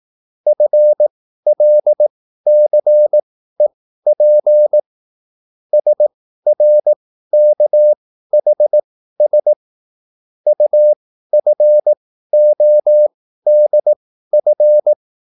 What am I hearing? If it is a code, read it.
Morse code, heard as a keyed tone, about 18 wpm, spaced as for 13 wpm, 600 Hz. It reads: FLCEP SRKHS UFODF